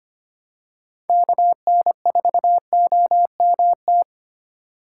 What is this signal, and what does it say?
Morse code, heard as a keyed tone, about 25 wpm, 700 Hz. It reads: KN4OMT
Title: USA Call Signs - 6 Characters - 3x Character Spacing - 25wpm